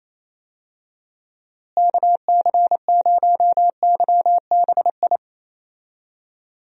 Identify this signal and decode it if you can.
Morse code, heard as a keyed tone, about 28 wpm, 700 Hz. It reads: KC0YBI
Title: USA Call Signs - 6 Characters - Repeated 2x - 28wpm